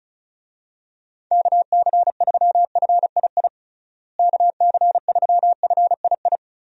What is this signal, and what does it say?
Morse code, heard as a keyed tone, about 34 wpm, 700 Hz. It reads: KC3FII KC3FII
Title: USA Call Signs - 6 Characters - Repeated 2x - 35wpm